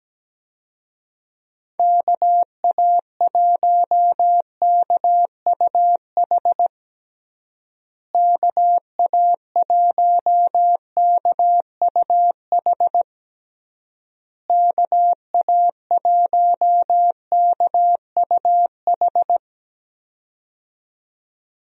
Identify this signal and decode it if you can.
Morse code, heard as a keyed tone, about 17 wpm, 700 Hz. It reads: KA1KUH KA1KUH KA1KUH
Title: USA Call Signs - 6 Characters - Repeated 3x - 17wpm